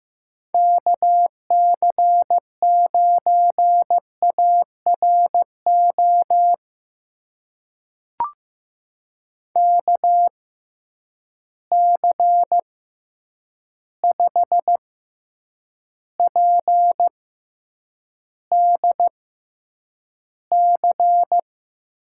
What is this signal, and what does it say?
Morse code, heard as a keyed tone, about 15 wpm, spaced as for 8 wpm, 700 Hz. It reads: KC9ARO KC5PDC
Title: USA Call Signs - 6 Characters - 6x Character Spacing - 15wpm